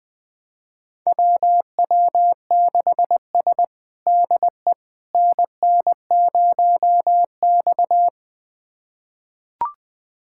Morse code, heard as a keyed tone, about 20 wpm, 700 Hz. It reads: WW6S DE NN0X
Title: USA Callsign DE Callsign - 4 Characters - 20wpm